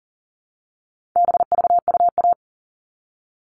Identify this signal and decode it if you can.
Morse code, heard as a keyed tone, about 40 wpm, 700 Hz. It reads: BVUA
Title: All Cut Numbers - 4-Digit 40wpm